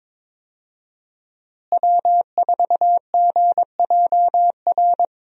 Morse code, heard as a keyed tone, about 21 wpm, 700 Hz. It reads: W4GJR
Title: USA Call Signs - 5 Characters - Repeated 3x - 22wpm